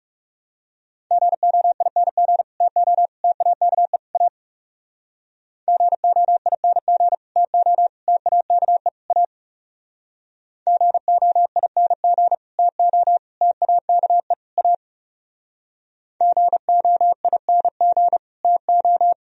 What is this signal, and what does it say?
Morse code, heard as a keyed tone, about 37 wpm, 700 Hz. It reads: GOING TO TAKE A GOING TO TAKE A GOING TO TAKE A GOING TO
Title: Speed-Racing - Sets of 4 Words 30wpm